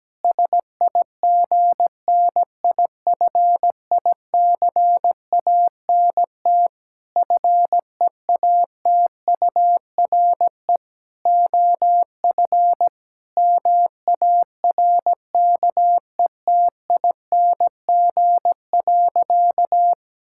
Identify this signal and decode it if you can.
Morse code, heard as a keyed tone, about 17 wpm, 700 Hz. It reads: SIGNIFICANT FEATURE OF MARKETING.
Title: Sentences from Top 1000 Words - Review Part 6 - 17wpm